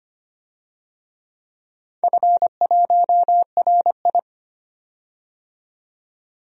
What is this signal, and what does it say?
Morse code, heard as a keyed tone, about 25 wpm, 700 Hz. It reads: F1RI
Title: DXCC-100 Call Signs - 4 Characters - 25wpm